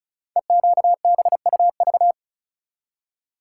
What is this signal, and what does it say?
Morse code, heard as a keyed tone, about 36 wpm, 700 Hz. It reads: EQBUV
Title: Code Groups - 5 Letters - 35wpm